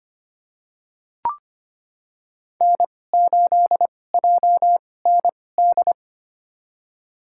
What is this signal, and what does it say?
Morse code, heard as a keyed tone, about 25 wpm, spaced as for 18 wpm, 700 Hz. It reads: N8JND
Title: USA Call Signs - 5 Characters - 2x Character Spacing - 25wpm